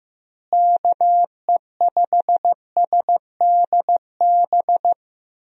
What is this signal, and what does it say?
Morse code, heard as a keyed tone, about 15 wpm, 700 Hz. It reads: KE5SDB
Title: USA Call Signs - 6 Characters - Repeated 2x - 15wpm